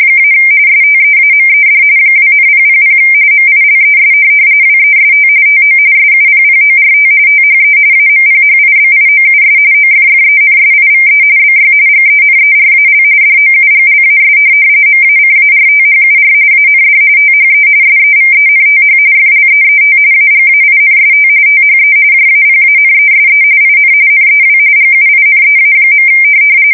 RTTY, about 45.45 baud. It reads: - STEAMED 2,000 MILES WEST TO SUPPORT THE AMPHIBIOUS OFFENSIVE BY GENERAL DOUGLAS MACARTHUR AND THE SEVENTH FLEET AGAINST THE JAPANESE IN NEW GUINEA.
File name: fsk170 / USS-New-Jersey